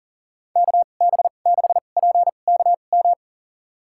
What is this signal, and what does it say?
Morse code, heard as a keyed tone, about 40 wpm, spaced as for 29 wpm, 700 Hz. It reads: KB6PKM
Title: USA Call Signs - 6 Characters - 2x Character Spacing - 40wpm